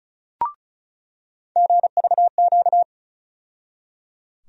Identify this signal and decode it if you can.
Morse code, heard as a keyed tone, about 36 wpm, 700 Hz. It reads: GVQ
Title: USA Call Sign Postfixes - 3 Letters 35wpm